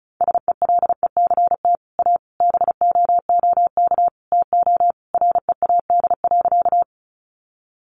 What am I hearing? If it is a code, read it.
Morse code, heard as a keyed tone, about 34 wpm, 700 Hz. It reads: SELECT A BOOK TO READ.
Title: Sentences from Top 1300 Words 35wpm